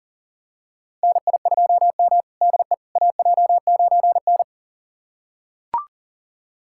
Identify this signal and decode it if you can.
Morse code, heard as a keyed tone, about 38 wpm, 700 Hz. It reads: NI2M DE AJ9N
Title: USA Callsign DE Callsign - 4 Characters - 40wpm